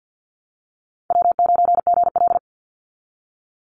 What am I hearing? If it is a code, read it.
Morse code, heard as a keyed tone, about 50 wpm, 700 Hz. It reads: W9GL
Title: Easy Call Signs - 50wpm